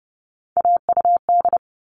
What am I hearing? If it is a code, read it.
Morse code, heard as a keyed tone, about 30 wpm, 700 Hz. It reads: AUD
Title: All Cut Numbers - 3-Digit 30wpm